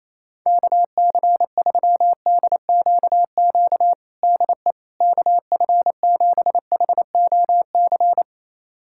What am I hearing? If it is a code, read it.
Morse code, heard as a keyed tone, about 28 wpm, 700 Hz. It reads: KC3DQQ DE KF7HOC
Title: USA Callsign DE Callsign 28wpm